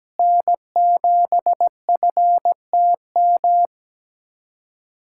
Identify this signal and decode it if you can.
Morse code, heard as a keyed tone, about 17 wpm, 700 Hz. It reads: N7FTM
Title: USA Callsign DE Callsign 17wpm